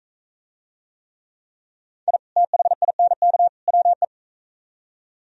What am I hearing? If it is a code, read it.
Morse code, heard as a keyed tone, about 41 wpm, 700 Hz. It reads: I THINK WE